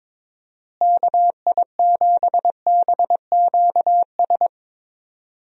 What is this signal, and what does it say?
Morse code, heard as a keyed tone, about 22 wpm, 700 Hz. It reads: KI7BQS
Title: USA Call Signs - 6 Characters - Repeated 3x - 22wpm